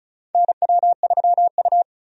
Morse code, heard as a keyed tone, about 36 wpm, 700 Hz. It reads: NW3U
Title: USA Call Signs - 4 Characters - 35wpm